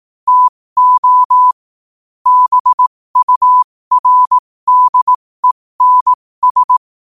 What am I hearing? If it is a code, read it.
Morse code, heard as a keyed tone, about 18 wpm, spaced as for 15 wpm, 1000 Hz. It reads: TO BURDENS